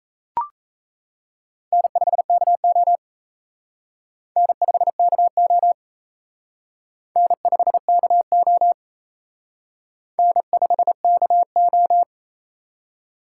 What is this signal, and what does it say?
Morse code, heard as a keyed tone, about 35 wpm, 700 Hz. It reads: N5KO N5KO N5KO N5KO